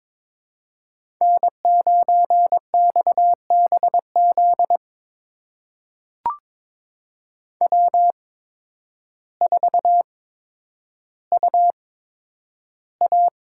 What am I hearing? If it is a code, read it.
Morse code, heard as a keyed tone, about 22 wpm, spaced as for 10 wpm, 700 Hz. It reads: N9XBZ W4UA
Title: USA Call Signs - 5 Characters - 8x Character Spacing - 22wpm